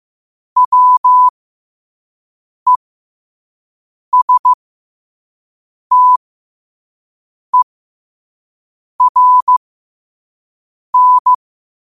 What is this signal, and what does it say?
Morse code, heard as a keyed tone, about 15 wpm, spaced as for 5 wpm, 1000 Hz. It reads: WESTERN
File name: qotdfiles / 5wpm